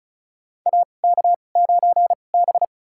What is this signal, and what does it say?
Morse code, heard as a keyed tone, about 36 wpm, spaced as for 25 wpm, 700 Hz. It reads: AK9B
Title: USA Call Signs - 4 Characters - 2x Character Spacing - 35wpm